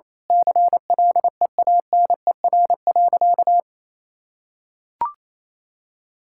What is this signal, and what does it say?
Morse code, heard as a keyed tone, about 27 wpm, 700 Hz. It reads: CLEANER.
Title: Pangram Challenge v2 - 28wpm